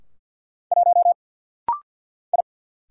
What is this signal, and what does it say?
Morse code, heard as a keyed tone, about 50 wpm, 700 Hz. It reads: 1 I
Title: Single Letter-Number - Rapid-Fire 50wpm